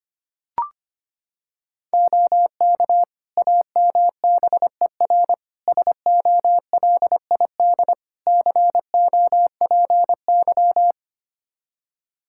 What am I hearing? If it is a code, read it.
Morse code, heard as a keyed tone, about 25 wpm, 700 Hz. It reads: OK AMBER SOLID COPY
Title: Practice QSO - Exchange 3 - More Info - 25wpm